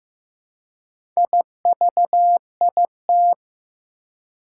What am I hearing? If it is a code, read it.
Morse code, heard as a keyed tone, about 15 wpm, 700 Hz. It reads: IVIT
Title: Four Challenging Letter-Number Combinations 15wpm